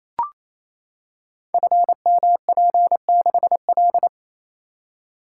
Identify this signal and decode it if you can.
Morse code, heard as a keyed tone, about 29 wpm, 700 Hz. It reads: FMP6L
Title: Code Groups - 5 Letter:Numbers - 28wpm